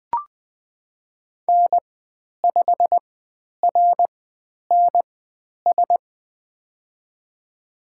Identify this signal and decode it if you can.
Morse code, heard as a keyed tone, about 20 wpm, spaced as for 10 wpm, 700 Hz. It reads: N5RNS
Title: 095 Lesson 16g - Letter c - Call Signs 20wpm